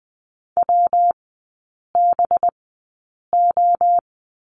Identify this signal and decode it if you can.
Morse code, heard as a keyed tone, about 20 wpm, spaced as for 8 wpm, 700 Hz. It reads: WBO
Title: 212 Lesson 34e - Number 8 - Three Characters 20wpm